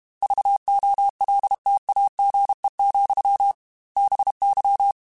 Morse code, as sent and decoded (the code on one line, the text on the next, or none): ...- --- .-.. - .- --. --..-- / -... -.--
VOLTAGE, BY